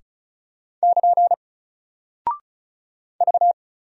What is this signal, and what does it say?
Morse code, heard as a keyed tone, about 36 wpm, 700 Hz. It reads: <KN> V
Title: Single Character (Letter:Number:Punctuation:Prosign) Alt 35wpm